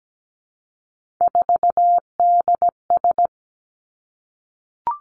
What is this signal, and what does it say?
Morse code, heard as a keyed tone, about 17 wpm, 700 Hz. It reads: 4DS
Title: Three Challenging Letter-Number Combinations 17wpm